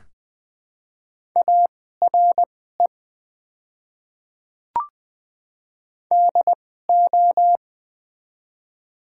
Morse code, heard as a keyed tone, about 20 wpm, spaced as for 14 wpm, 700 Hz. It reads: ARE DO
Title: 2x Character Spacing - Top 100 Words - 20wpm